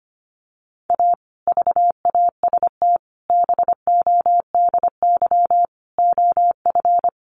Morse code, heard as a keyed tone, about 25 wpm, 700 Hz. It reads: A VAST BODY OF
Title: Sentences from Top 1300 Words 25wpm